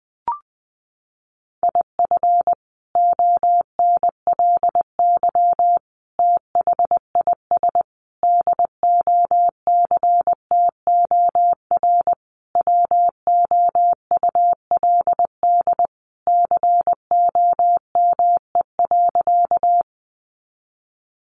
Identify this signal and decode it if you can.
Morse code, heard as a keyed tone, about 20 wpm, 700 Hz. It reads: IF ONLY THIS DOCTOR WOULD COME.
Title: Sentences from Top 1500 Words 20wpm